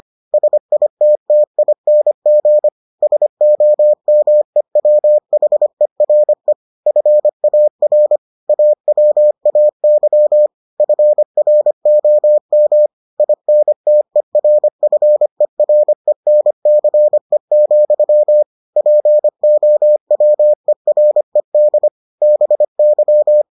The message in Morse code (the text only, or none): SITTING SOMEWHERE FAR AWAY FROM INTERFERENCE, POWERED BY